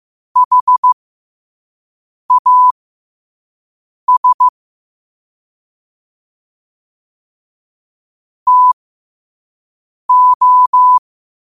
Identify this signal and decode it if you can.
Morse code, heard as a keyed tone, about 15 wpm, spaced as for 5 wpm, 1000 Hz. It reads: HAS TO
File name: qotdfiles / 5wpm